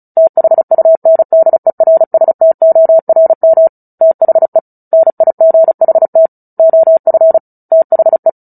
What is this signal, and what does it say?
Morse code, heard as a keyed tone, about 34 wpm, 650 Hz. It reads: THUNDERSTORM THE NIGHT OF THE
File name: Book Two / 09-Chapter 9 - 35 WPM